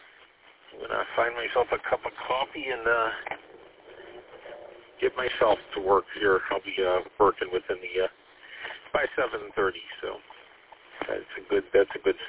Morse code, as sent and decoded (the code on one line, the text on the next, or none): none